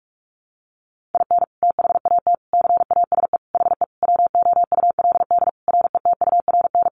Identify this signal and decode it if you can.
Morse code, heard as a keyed tone, about 44 wpm, 700 Hz. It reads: IN THAT CASE HE WOULD RETURN
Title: Sentences from Top 800 Words 45wpm